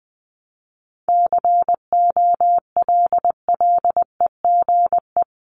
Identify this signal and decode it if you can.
Morse code, heard as a keyed tone, about 20 wpm, 700 Hz. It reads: COLLEGE
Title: Top 1001-1250 Words 20wpm